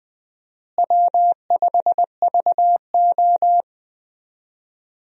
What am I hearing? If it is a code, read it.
Morse code, heard as a keyed tone, about 20 wpm, 700 Hz. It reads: W5VO
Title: USA Call Signs - 4 Characters - Repeated 2x - 20wpm